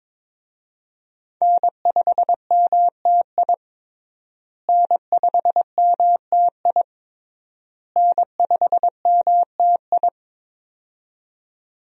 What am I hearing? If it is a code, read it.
Morse code, heard as a keyed tone, about 22 wpm, 690 Hz. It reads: N5MTI N5MTI N5MTI
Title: USA Call Signs - 5 Characters - Repeated 3x - 22wpm